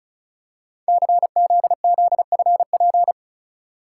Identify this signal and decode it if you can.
Morse code, heard as a keyed tone, about 34 wpm, 700 Hz. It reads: CZZFP
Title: Code Groups - 5 Letter:Numbers - 35wpm